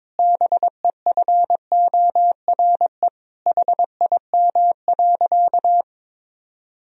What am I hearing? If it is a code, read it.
Morse code, heard as a keyed tone, about 21 wpm, 700 Hz. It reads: BEFORE HIM.
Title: Sentences from Top 1000 Words - Review Part 2 - 22wpm